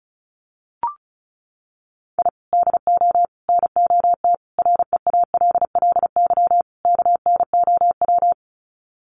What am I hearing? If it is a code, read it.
Morse code, heard as a keyed tone, about 34 wpm, 700 Hz. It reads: I DO NOT REALLY KNOW
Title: Sets of 5 Words Encore 35wpm